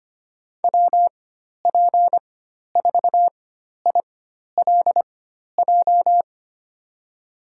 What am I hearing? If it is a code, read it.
Morse code, heard as a keyed tone, about 25 wpm, spaced as for 12 wpm, 700 Hz. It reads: WP4ILJ
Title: USA Call Signs - 6 Characters - 4x Character Spacing - 25wpm